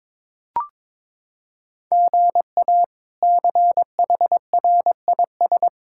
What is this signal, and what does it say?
Morse code, heard as a keyed tone, about 22 wpm, 700 Hz. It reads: GA CHRIS
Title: K1USN SST QSO Practice - 22wpm